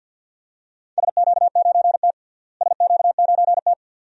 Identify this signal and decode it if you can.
Morse code, heard as a keyed tone, about 50 wpm, 700 Hz. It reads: SQ9T SQ9T
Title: DXCC-100 Call Signs - 4 Characters - Repeated 2x - 50wpm